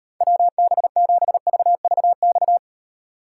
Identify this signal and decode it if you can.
Morse code, heard as a keyed tone, about 38 wpm, 700 Hz. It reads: WB7VVX